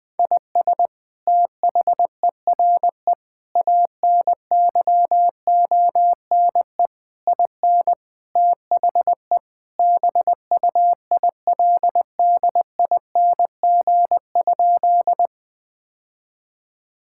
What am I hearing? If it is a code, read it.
Morse code, heard as a keyed tone, about 21 wpm, 700 Hz. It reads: IS THERE ANYONE IN THE BUILDING?
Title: Sentences from Top 1000 Words - Review Part 5 - 20wpm